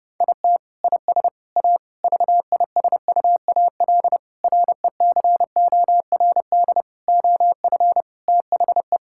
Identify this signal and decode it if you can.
Morse code, heard as a keyed tone, about 30 wpm, 700 Hz. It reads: IT IS A VISUAL RECORD OF THE
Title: Sentences from Top 1000 Words - Review Part 6 - 30wpm